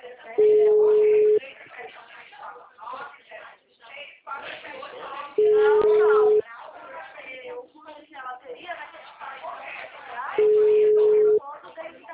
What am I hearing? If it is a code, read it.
Ringback tone (Europe, 425 Hz), with speech.
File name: out- -2025- - - .40852